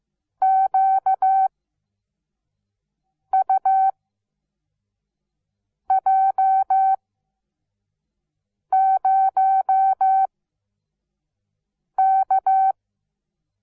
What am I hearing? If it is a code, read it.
Morse code, heard as a keyed tone, about 15 wpm, spaced as for 4 wpm, 770 Hz. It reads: QUJ0K
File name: Onlinelehrgang / morsen / DJ4UF Morsen17